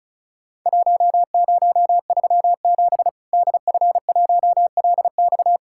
Morse code, heard as a keyed tone, about 36 wpm, 700 Hz. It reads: 1037 DF1LX
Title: Worked All Europe DX Contest - QTC Exchange - 35wpm